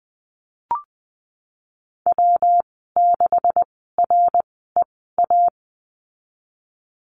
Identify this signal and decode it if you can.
Morse code, heard as a keyed tone, about 20 wpm, spaced as for 15 wpm, 700 Hz. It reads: W6REA
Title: 128 Lesson 21g - Number 6 - Call Signs 20wpm